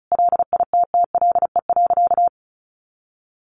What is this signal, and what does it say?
Morse code, heard as a keyed tone, about 34 wpm, 700 Hz. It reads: LITTLE.
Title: Sentences from Top 1500 Words 35wpm